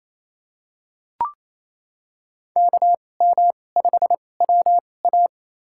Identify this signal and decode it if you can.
Morse code, heard as a keyed tone, about 28 wpm, spaced as for 20 wpm, 700 Hz. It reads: KM5WA
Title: USA Call Signs - 2x Character Spacing - 28wpm